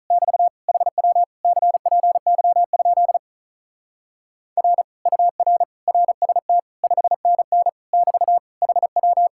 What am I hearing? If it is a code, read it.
Morse code, heard as a keyed tone, about 44 wpm, 700 Hz. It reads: <BT> HW CPY? R UR RST 5NN <BT> HW